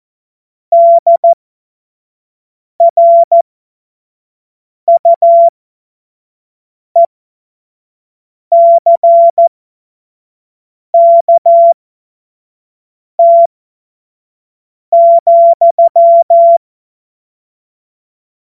Morse code, heard as a keyed tone, about 14 wpm, spaced as for 5 wpm, 680 Hz. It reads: DRUECKT,
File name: Morsekurs / Witze / Witz 02 April 2024 25 F70